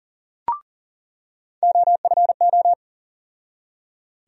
Morse code, heard as a keyed tone, about 40 wpm, 700 Hz. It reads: OFO
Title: USA Call Sign Postfixes - 3 Letters 40wpm